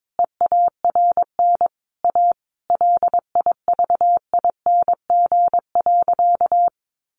Morse code, heard as a keyed tone, about 22 wpm, 700 Hz. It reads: EARN A LIVING.
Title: Sentences from Top 1500 Words 22wpm